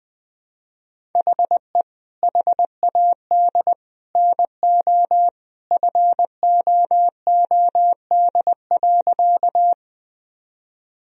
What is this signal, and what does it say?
Morse code, heard as a keyed tone, about 21 wpm, 700 Hz. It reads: HE HAD NO FOOD.
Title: Sentences from Top 1000 Words - Review Part 1 - 20wpm